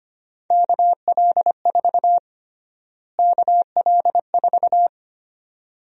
Morse code, heard as a keyed tone, about 25 wpm, 700 Hz. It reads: KL4 KL4
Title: USA Call Sign Prefixes - Repeated 2x - 25wpm